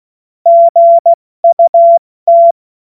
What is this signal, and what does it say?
Morse code, heard as a keyed tone, about 16 wpm, spaced as for 14 wpm, 680 Hz. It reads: GUT
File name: Morsekurs / Witze / Witz 30 Januar 2024 70 F80